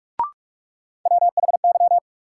Morse code, heard as a keyed tone, about 45 wpm, 700 Hz. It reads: WHY
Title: Sentences from Top 200 Words - New Word Drill - 45wpm